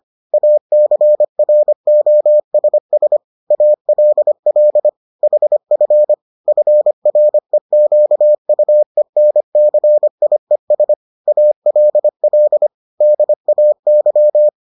Morse code, heard as a keyed tone, about 25 wpm, 600 Hz. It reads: ACROSS ALL HF FREQUENCIES ALL DAY